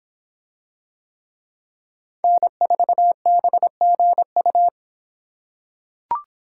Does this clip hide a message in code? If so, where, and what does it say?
Yes: Morse code from 2.2 to 4.7 s, heard as a keyed tone, about 26 wpm, 700 Hz. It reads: N4BGU